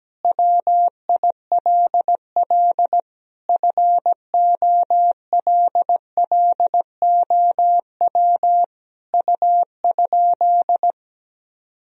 Morse code, heard as a keyed tone, about 17 wpm, 700 Hz. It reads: WILL FOLLOW U?
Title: Sentences from Top 200 Words and Abbreviations 17wpm